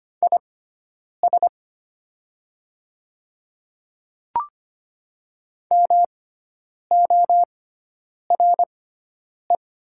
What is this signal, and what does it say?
Morse code, heard as a keyed tone, about 25 wpm, spaced as for 9 wpm, 700 Hz. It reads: IS MORE
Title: 6x Character Spacing - Top 100 Words - 25wpm